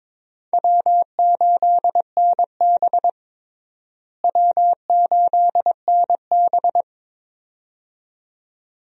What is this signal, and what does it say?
Morse code, heard as a keyed tone, about 22 wpm, 700 Hz. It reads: W8NB W8NB
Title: USA Call Signs - 4 Characters - Repeated 2x - 22wpm